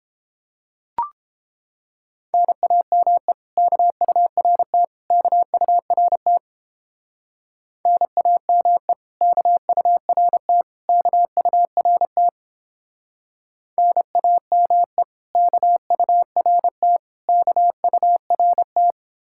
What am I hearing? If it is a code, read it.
Morse code, heard as a keyed tone, about 29 wpm, 700 Hz. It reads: NAME KURT KURT NAME KURT KURT NAME KURT KURT